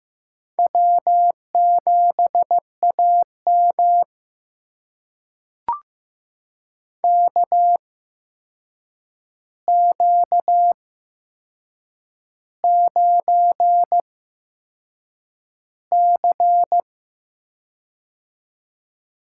Morse code, heard as a keyed tone, about 15 wpm, spaced as for 6 wpm, 700 Hz. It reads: W7AM KQ9C
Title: USA Call Signs - 4 Characters - 8x Character Spacing - 15wpm